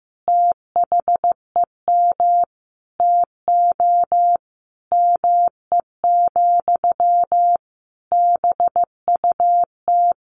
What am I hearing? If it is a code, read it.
Morse code, heard as a keyed tone, about 15 wpm, 700 Hz. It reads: THEM TO ME, BUT
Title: Sentences from Top 1500 Words 15wpm